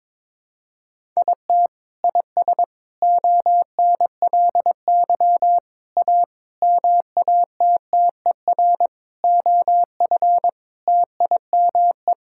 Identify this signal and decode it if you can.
Morse code, heard as a keyed tone, about 22 wpm, 700 Hz. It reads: IT IS ONLY A MATTER OF TIME
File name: practice / 22wpm / Sets of 8 Words - 22wpm